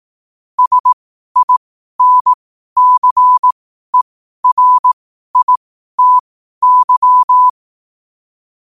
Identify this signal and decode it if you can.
Morse code, heard as a keyed tone, about 18 wpm, spaced as for 12 wpm, 1000 Hz. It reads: SINCERITY
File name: qotdfiles / 12wpm